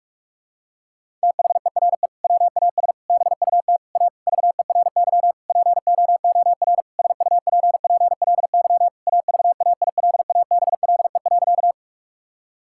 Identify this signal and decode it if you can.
Morse code, heard as a keyed tone, about 46 wpm, 700 Hz. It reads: THERE WAS BUT A VERY POOR SUPPLY AVAILABLE.
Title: Sentences from Top 1000 Words - Review Part 5 - 45wpm